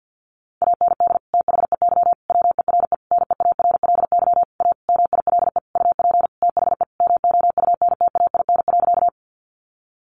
Morse code, heard as a keyed tone, about 48 wpm, 700 Hz. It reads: AND THEY WERE NEARLY A MILE UP THE MOUNTAIN.
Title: Sentences from Top 1300 Words 50wpm